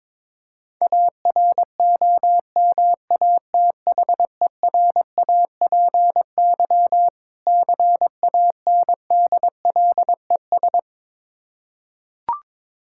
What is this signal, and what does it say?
Morse code, heard as a keyed tone, about 23 wpm, 690 Hz. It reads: AROMATHERAPY CANDLES
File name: Difficult Words with Hints - 22wpm